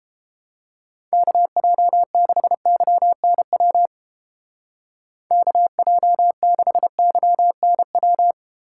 Morse code, heard as a keyed tone, about 29 wpm, 700 Hz. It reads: KJ6YNW KJ6YNW